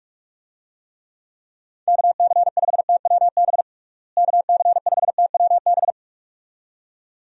Morse code, heard as a keyed tone, about 46 wpm, 700 Hz. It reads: KC5TWB KC5TWB
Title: USA Call Signs - 6 Characters - Repeated 2x - 45wpm